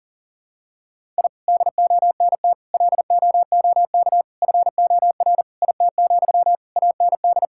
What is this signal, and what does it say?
Morse code, heard as a keyed tone, about 40 wpm, 700 Hz. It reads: I DONT LOOK FOR IT, AND
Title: Sentences from Top 1000 Words - Review Part 5 - 40wpm